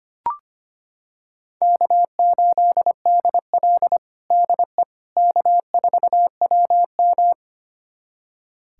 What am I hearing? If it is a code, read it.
Morse code, heard as a keyed tone, about 25 wpm, 700 Hz. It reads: K8DL DE K4WM
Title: USA Callsign DE Callsign - 4 Characters - 25wpm